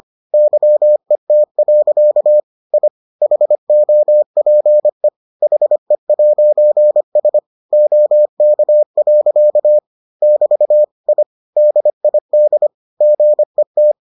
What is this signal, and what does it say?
Morse code, heard as a keyed tone, about 25 wpm, 600 Hz. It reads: YET. I HOPE HE'S OK. <BT> I DID GET